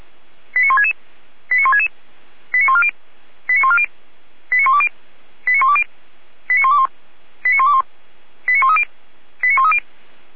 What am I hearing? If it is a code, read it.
Five-tone selective call (ZVEI), heard as quick runs of five tones, tones of about 70 ms. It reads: 89166 89166 89140 89140 89120 89120 89121 89121 89130 89130